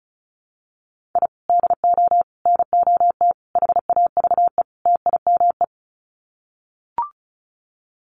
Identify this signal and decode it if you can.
Morse code, heard as a keyed tone, about 34 wpm, 700 Hz. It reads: I DO NOT HAVE TIME
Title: Sets of 5 Words Encore 35wpm